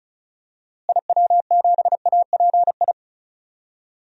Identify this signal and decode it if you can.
Morse code, heard as a keyed tone, about 34 wpm, 700 Hz. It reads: IW7API
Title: DXCC-100 Call Signs - 6 Characters - Repeated 3x - 35wpm